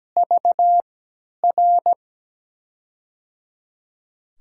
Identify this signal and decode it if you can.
Morse code, heard as a keyed tone, about 17 wpm, spaced as for 10 wpm, 700 Hz. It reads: VR
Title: USA Call Signs - 6 Characters - 3x Character Spacing - 17wpm